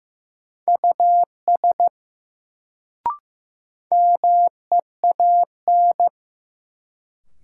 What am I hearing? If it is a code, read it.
Morse code, heard as a keyed tone, about 15 wpm, 700 Hz. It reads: US MEAN
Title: Sentences from Top 200 Words - New Word Drill - 15wpm